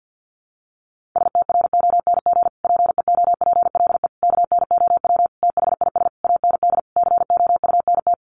Morse code, heard as a keyed tone, about 50 wpm, 700 Hz. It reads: STRONG PEOPLE KNOW THIS AND COUNT
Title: Sentences from Top 1500 Words 50wpm